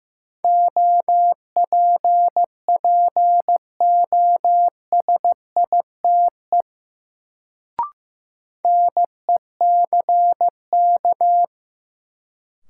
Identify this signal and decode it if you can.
Morse code, heard as a keyed tone, about 15 wpm, 700 Hz. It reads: OPPOSITE NECK
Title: Sets of 5 Words Encore - New Word Drill - 15wpm